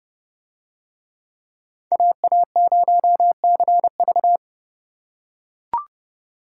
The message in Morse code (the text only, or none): AA0CV